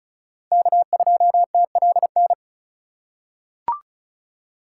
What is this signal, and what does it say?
Morse code, heard as a keyed tone, about 34 wpm, 700 Hz. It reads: K2TLN
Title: USA Call Signs - 5 Characters - 3x Character Spacing - 35wpm